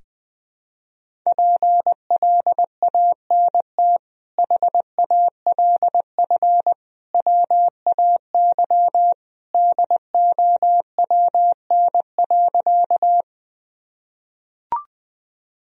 Morse code, heard as a keyed tone, about 21 wpm, 700 Hz. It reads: PLANT HALF WAY DOWN.
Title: Sentences from Top 1000 Words - Review Part 1 - 20wpm